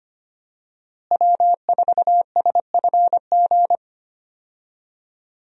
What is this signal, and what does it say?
Morse code, heard as a keyed tone, about 25 wpm, 700 Hz. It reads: W4SFG
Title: USA Call Signs - 5 Characters - Repeated 2x - 25wpm